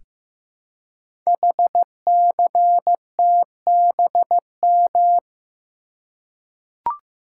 Morse code, heard as a keyed tone, about 15 wpm, 700 Hz. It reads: HCTBM
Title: Code Groups - 5 Letters - 15wpm